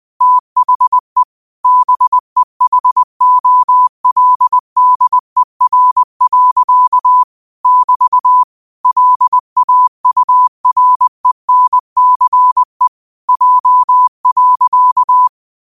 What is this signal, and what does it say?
Morse code, heard as a keyed tone, about 20 wpm, 1000 Hz. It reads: THE BEHOLDER. <BT> LAURENCE J.
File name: qotdfiles / 20wpm